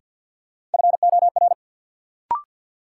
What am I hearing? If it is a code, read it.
Morse code, heard as a keyed tone, about 48 wpm, 700 Hz. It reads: FOR
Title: Sentences from Top 100 Words - New Word Drill - 50wpm